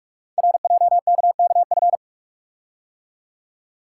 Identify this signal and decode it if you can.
Morse code, heard as a keyed tone, about 44 wpm, 700 Hz. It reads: RJKKF
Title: Code Groups - 5 Letters - 45wpm